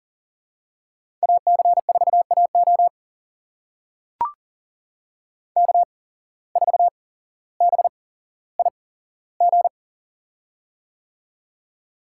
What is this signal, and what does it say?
Morse code, heard as a keyed tone, about 41 wpm, spaced as for 17 wpm, 700 Hz. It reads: AC4AO K4BIG
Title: USA Call Signs - 5 Characters - 8x Character Spacing - 40wpm